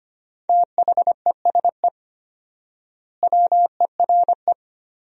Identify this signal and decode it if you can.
Morse code, heard as a keyed tone, about 25 wpm, 700 Hz. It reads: THESE WERE
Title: 4x Word-Spacing - Sentences from Top 300 Words - 25wpm